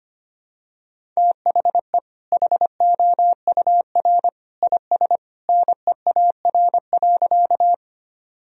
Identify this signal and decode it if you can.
Morse code, heard as a keyed tone, about 25 wpm, 700 Hz. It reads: THE HOUR IS NEAR.
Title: Sentences from Top 1000 Words - Review Part 1 - 25wpm